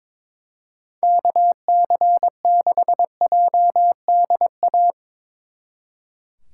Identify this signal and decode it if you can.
Morse code, heard as a keyed tone, about 24 wpm, 700 Hz. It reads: KC6JDA
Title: Speed Racing - USA 2x3 Call Signs 22wpm